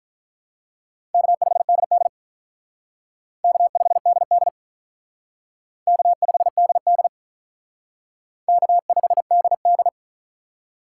Morse code, heard as a keyed tone, about 46 wpm, 700 Hz. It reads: K5DD K5DD K5DD K5DD